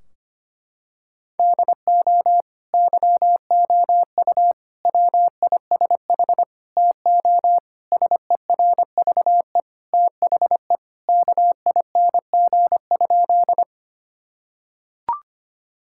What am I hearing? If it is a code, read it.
Morse code, heard as a keyed tone, about 25 wpm, 700 Hz. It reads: DO YOU WISH TO SERVE THE KING?
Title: Sentences from Top 1600 Words 25wpm